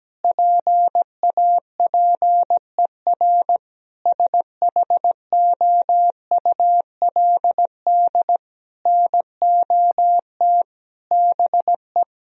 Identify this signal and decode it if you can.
Morse code, heard as a keyed tone, about 17 wpm, 700 Hz. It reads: PAPER SHOULD NOT BE
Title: Sentences from Top 1000 Words - Review Part 3 - 17wpm